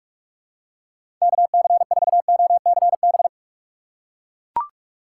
Morse code, heard as a keyed tone, about 42 wpm, 700 Hz. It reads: KC4OCB
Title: USA Call Signs - 6 Characters - 4x Character Spacing - 45wpm